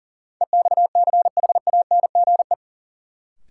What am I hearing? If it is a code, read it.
Morse code, heard as a keyed tone, about 40 wpm, 700 Hz. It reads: EXCHANGE
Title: Sentences from Top 1000 Words - Review Part 6 - New Word Drill - 40wpm